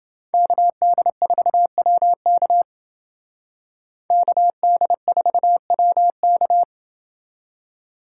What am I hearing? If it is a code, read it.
Morse code, heard as a keyed tone, about 28 wpm, 700 Hz. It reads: KD4WK KD4WK